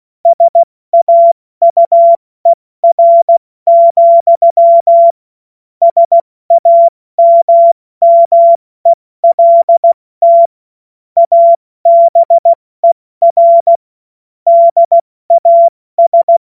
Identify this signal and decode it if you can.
Morse code, heard as a keyed tone, about 16 wpm, 680 Hz. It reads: SAUER, SAMMELT ABER DAS
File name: Morsekurs / Witze / Witz 11. Mai 2021, 70 F80